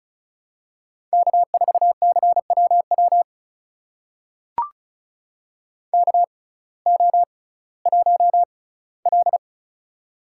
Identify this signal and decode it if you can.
Morse code, heard as a keyed tone, about 34 wpm, spaced as for 19 wpm, 700 Hz. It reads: K4CWW KO1L
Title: USA Call Signs - 6x Character Spacing - 35wpm